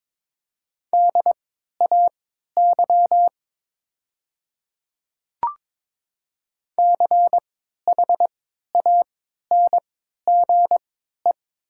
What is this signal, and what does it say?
Morse code, heard as a keyed tone, about 22 wpm, spaced as for 12 wpm, 700 Hz. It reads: DAY CHANGE
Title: 3x Character Spacing - Top 200 Words - 22wpm